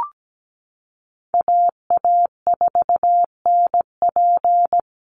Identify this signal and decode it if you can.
Morse code, heard as a keyed tone, about 17 wpm, 700 Hz. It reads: AA4NP
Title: Call Signs Repeated 2x - 17wpm